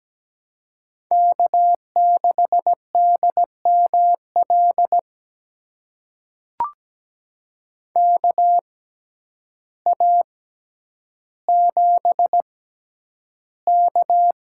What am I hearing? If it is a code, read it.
Morse code, heard as a keyed tone, about 17 wpm, spaced as for 9 wpm, 700 Hz. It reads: K6DML KA7K
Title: USA Call Signs - 6x Character Spacing - 17wpm